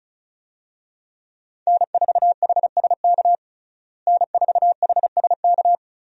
Morse code, heard as a keyed tone, about 36 wpm, 700 Hz. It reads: N4HSK N4HSK
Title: USA Call Signs - 5 Characters - Repeated 3x - 35wpm